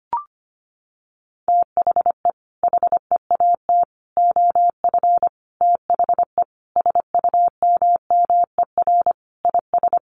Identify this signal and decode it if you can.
Morse code, heard as a keyed tone, about 25 wpm, 700 Hz. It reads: THE HEAT OF THE SUMMER IS
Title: Sentences from Top 1100 Words 25wpm